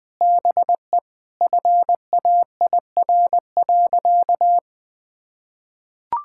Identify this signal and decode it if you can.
Morse code, heard as a keyed tone, about 21 wpm, 700 Hz. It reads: BE FAIR.
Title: Sentences with 3-Word Phrases - 20wpm